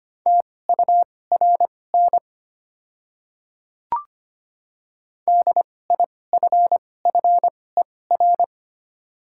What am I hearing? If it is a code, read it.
Morse code, heard as a keyed tone, about 25 wpm, spaced as for 18 wpm, 700 Hz. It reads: TURN DIFFER
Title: 2x Character Spacing - Top 200 Words - 25wpm